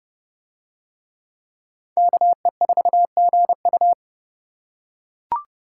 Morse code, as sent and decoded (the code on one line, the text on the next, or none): -.- . ....- --. ..-
KE4GU